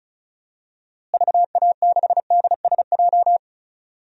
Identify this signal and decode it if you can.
Morse code, heard as a keyed tone, about 36 wpm, 700 Hz. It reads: VA6DSJ
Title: DXCC-100 Call Signs - 6 Characters - Repeated 2x - 35wpm